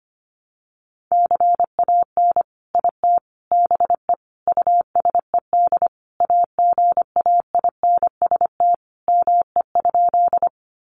Morse code, heard as a keyed tone, about 25 wpm, 700 Hz. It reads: CAN IT BE USED AGAINST ME?
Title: Sentences from Top 600 Words 25wpm